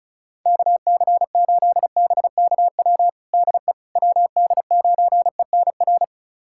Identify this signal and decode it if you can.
Morse code, heard as a keyed tone, about 34 wpm, 690 Hz. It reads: KC8BKW DE WD9ENR
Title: USA Callsign DE Callsign - 6 Characters - 35wpm